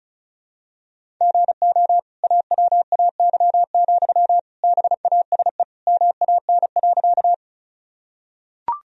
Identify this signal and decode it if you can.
Morse code, heard as a keyed tone, about 34 wpm, 700 Hz. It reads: GO AWAY, BASE MAN.
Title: Sentences from Top 1000 Words - Review Part 1 - 35wpm